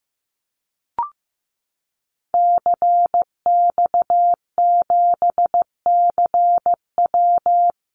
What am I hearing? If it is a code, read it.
Morse code, heard as a keyed tone, about 15 wpm, 700 Hz. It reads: CX7CW
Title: Call Signs Repeated 3x - 15wpm